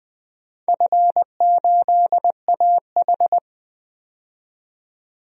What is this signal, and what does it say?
Morse code, heard as a keyed tone, about 20 wpm, 700 Hz. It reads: F8AH
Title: DXCC-100 Call Signs - 4 Characters - 20wpm